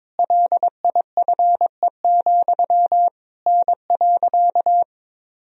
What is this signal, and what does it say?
Morse code, heard as a keyed tone, about 22 wpm, 700 Hz. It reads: LIFE, N.
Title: Sentences from Top 300 Words and Abbreviations 22wpm